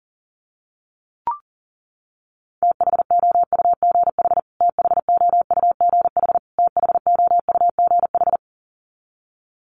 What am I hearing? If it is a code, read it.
Morse code, heard as a keyed tone, about 40 wpm, 700 Hz. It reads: THOUGH THOUGH THOUGH
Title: Top 500 Words Repeated 40wpm